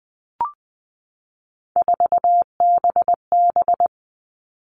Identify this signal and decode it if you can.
Morse code, heard as a keyed tone, about 20 wpm, 700 Hz. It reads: 4BB
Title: All Cut Numbers - 3-Digit 20wpm